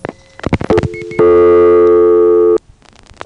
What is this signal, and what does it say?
Dial tone (North America), 350 + 440 Hz.
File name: phonetrips / SmyrnaGA5HQ